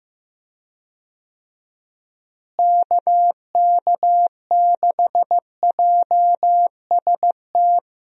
Morse code, heard as a keyed tone, about 15 wpm, 700 Hz. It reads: KK6JST
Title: USA Call Signs - 6 Characters - Repeated 2x - 15wpm